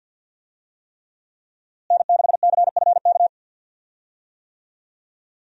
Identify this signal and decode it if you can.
Morse code, heard as a keyed tone, about 51 wpm, 700 Hz. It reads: N6CFK
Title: USA Call Signs - 5 Characters - Repeated 3x - 50wpm